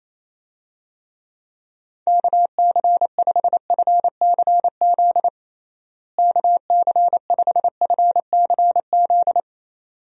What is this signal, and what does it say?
Morse code, heard as a keyed tone, about 27 wpm, 700 Hz. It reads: KC5FCZ KC5FCZ
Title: USA Call Signs - 6 Characters - Repeated 2x - 28wpm